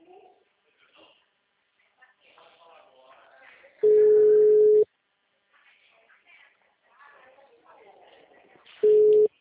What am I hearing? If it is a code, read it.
Ringback tone (Europe, 425 Hz), with speech.